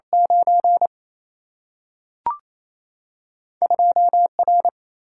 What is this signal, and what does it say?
Morse code, heard as a keyed tone, about 28 wpm, 700 Hz. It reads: 9 2 R
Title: Single-Letter Single-Number 28wpm